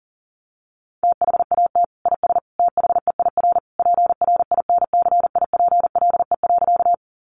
Morse code, heard as a keyed tone, about 40 wpm, 700 Hz. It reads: THAT IS THEIR PRINCIPLE.
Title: Sentences from Top 1400 Words 40wpm